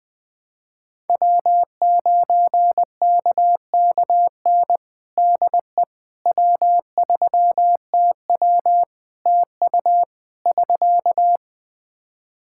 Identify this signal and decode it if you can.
Morse code, heard as a keyed tone, about 20 wpm, 700 Hz. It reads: W9KKN DE W3TW TU <SK>
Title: Practice QSO - Exchange 4 - Ending - 20wpm